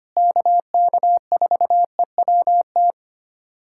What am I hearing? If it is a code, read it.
Morse code, heard as a keyed tone, about 25 wpm, 700 Hz. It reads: KK4EWT
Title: USA Callsign DE Callsign 25wpm